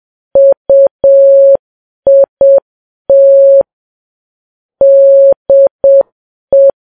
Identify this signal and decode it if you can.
Morse code, heard as a keyed tone, about 7 wpm, 550 Hz. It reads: UIT DE
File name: Files Web tls / Morse on9fv / morse7kps